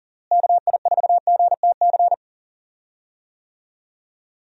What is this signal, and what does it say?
Morse code, heard as a keyed tone, about 37 wpm, 700 Hz. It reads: KI4GTC
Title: USA Call Signs - 6 Characters - Repeated 2x - 40wpm